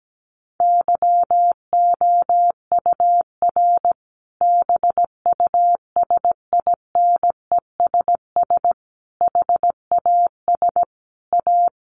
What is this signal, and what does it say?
Morse code, heard as a keyed tone, about 17 wpm, 700 Hz. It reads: YOUR BUSINESS HAS A
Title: Sentences from Top 1200 Words 17wpm